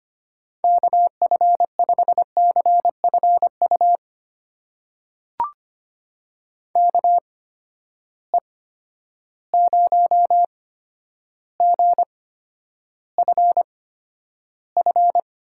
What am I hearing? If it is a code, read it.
Morse code, heard as a keyed tone, about 25 wpm, spaced as for 11 wpm, 700 Hz. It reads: KF5CFU KE0GFF
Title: USA Call Signs - 6 Characters - 8x Character Spacing - 25wpm